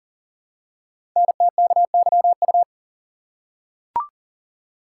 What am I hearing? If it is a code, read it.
Morse code, heard as a keyed tone, about 40 wpm, 700 Hz. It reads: NTKYU
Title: Code Groups - 5 Letters - 40wpm